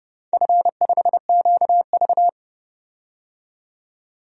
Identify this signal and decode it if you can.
Morse code, heard as a keyed tone, about 31 wpm, 700 Hz. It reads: F5QV
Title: DXCC-100 Call Signs - 4 Characters - 30wpm